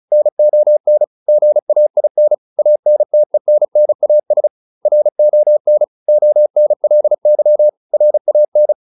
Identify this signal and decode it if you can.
Morse code, heard as a keyed tone, about 34 wpm, 600 Hz. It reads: NON GAIN ANTENNAS RON ONLY RAN